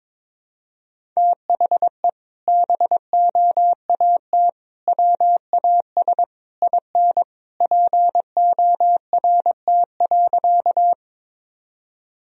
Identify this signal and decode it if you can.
Morse code, heard as a keyed tone, about 22 wpm, 700 Hz. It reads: THE BOAT WAS IN PORT.
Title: Sentences from Top 1000 Words - Review Part 1 - 22wpm